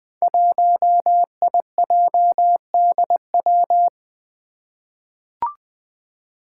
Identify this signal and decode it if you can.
Morse code, heard as a keyed tone, about 20 wpm, 700 Hz. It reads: 1IJDW
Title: Five Challenging Letter-Number Combinations 20wpm